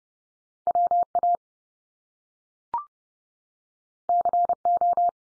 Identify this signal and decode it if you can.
Morse code, heard as a keyed tone, about 30 wpm, 700 Hz. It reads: WA CO
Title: US States 30WPM